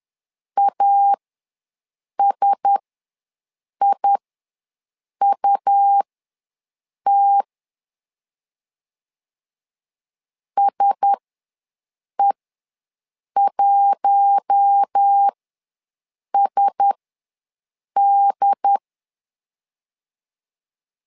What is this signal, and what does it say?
Morse code, heard as a keyed tone, about 10 wpm, spaced as for 6 wpm, 800 Hz. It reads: ASIUT SE1SD
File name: LEZIONI / 4 Q1S ws 10wpm 2208